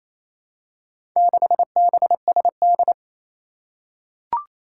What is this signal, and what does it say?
Morse code, heard as a keyed tone, about 27 wpm, 700 Hz. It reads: 6BSD
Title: Four Challenging Letter-Number Combinations 28wpm